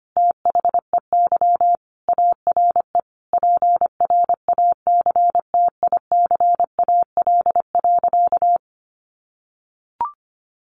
Morse code, heard as a keyed tone, about 25 wpm, 700 Hz. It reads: THEY ARE PRACTICAL.
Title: Sentences from Top 1400 Words 25wpm